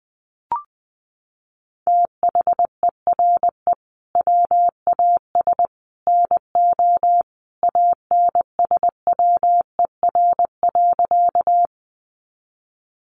Morse code, heard as a keyed tone, about 20 wpm, 700 Hz. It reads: THERE WAS NO ANSWER.
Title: Sentences from Top 300 Words 20wpm